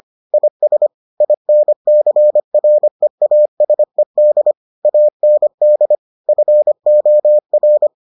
Morse code, heard as a keyed tone, about 25 wpm, 600 Hz. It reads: IS INCREASED AND FOR